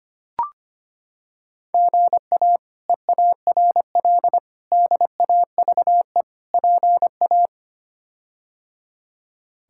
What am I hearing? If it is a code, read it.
Morse code, heard as a keyed tone, about 25 wpm, 700 Hz. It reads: GA EARL DAVE PA
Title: K1USN SST QSO Practice - 25wpm